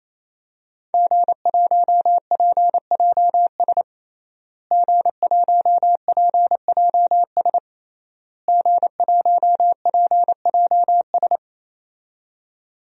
Morse code, heard as a keyed tone, about 27 wpm, 700 Hz. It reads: G1PJS G1PJS G1PJS
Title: DXCC-100 Call Signs - 5 Characters - Repeated 3x - 28wpm